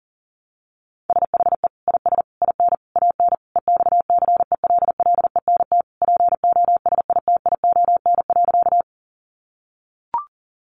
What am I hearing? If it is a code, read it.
Morse code, heard as a keyed tone, about 37 wpm, 700 Hz. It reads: SHE IS IN AN EXCELLENT POSITION.
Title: Sentences from Top 1000 Words 40wpm